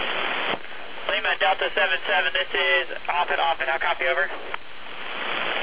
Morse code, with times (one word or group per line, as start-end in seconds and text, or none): none